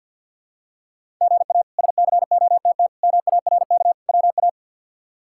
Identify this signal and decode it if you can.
Morse code, heard as a keyed tone, about 46 wpm, 700 Hz. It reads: GA SCOTT MARK WA
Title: K1USN SST QSO Practice - 50wpm